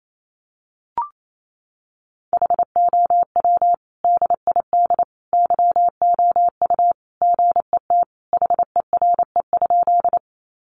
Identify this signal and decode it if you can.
Morse code, heard as a keyed tone, about 28 wpm, 700 Hz. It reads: HOW DID YOU GET HERE?
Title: Sentences from Top 200 Words 28wpm